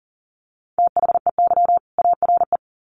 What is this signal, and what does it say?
Morse code, heard as a keyed tone, about 39 wpm, 700 Hz. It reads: THEY ARE
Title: Sets of 2 Words 40wpm